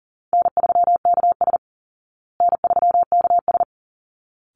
Morse code, heard as a keyed tone, about 40 wpm, 700 Hz. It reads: N3KS N3KS
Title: Call Signs Repeated 3x - 40wpm